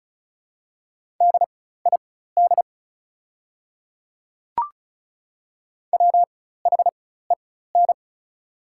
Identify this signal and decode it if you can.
Morse code, heard as a keyed tone, about 36 wpm, spaced as for 16 wpm, 700 Hz. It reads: DID WHEN
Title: 4x Character Spacing - Top 500 Words - 35wpm